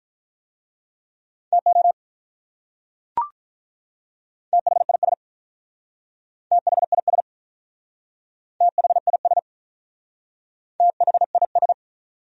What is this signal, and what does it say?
Morse code, heard as a keyed tone, about 44 wpm, 700 Hz. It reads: TO THIS THIS THIS THIS